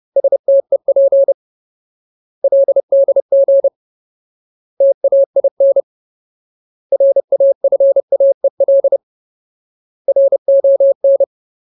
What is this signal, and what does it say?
Morse code, heard as a keyed tone, about 30 wpm, 550 Hz. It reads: STEP LDG TAIN RAFAEL RON
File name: Ultimate Word List / 30WPM Ultimate